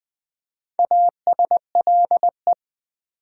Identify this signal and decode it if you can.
Morse code, heard as a keyed tone, about 20 wpm, 700 Hz. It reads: ASLE
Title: Build A Word - 2 Letters At A Time - 6-Letter Words - 20wpm